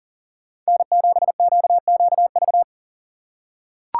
Morse code, heard as a keyed tone, about 40 wpm, 700 Hz. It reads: N7QQV
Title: USA Call Signs - 5 Characters - 4x Character Spacing - 40wpm